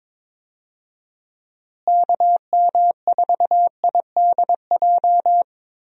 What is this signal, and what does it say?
Morse code, heard as a keyed tone, about 22 wpm, 700 Hz. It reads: KM4IDJ
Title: USA Call Signs - 6 Characters - Repeated 3x - 22wpm